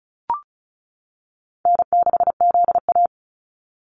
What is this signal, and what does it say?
Morse code, heard as a keyed tone, about 36 wpm, 700 Hz. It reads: N6ZA
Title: NA SOTA Activator Callsigns 35wpm